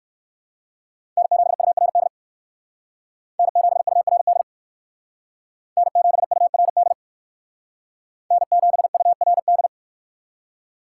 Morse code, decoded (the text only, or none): N7URD N7URD N7URD N7URD